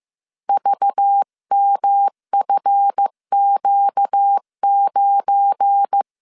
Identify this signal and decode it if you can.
Morse code, heard as a keyed tone, about 15 wpm, 800 Hz. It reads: VMFQ9
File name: CW / DET / 1 9 1500C S C 14wpm 3245